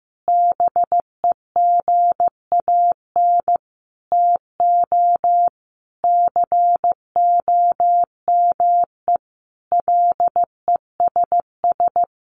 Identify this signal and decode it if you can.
Morse code, heard as a keyed tone, about 14 wpm, 700 Hz. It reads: BEGAN TO COME LESS
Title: Sentences from Top 1300 Words 15wpm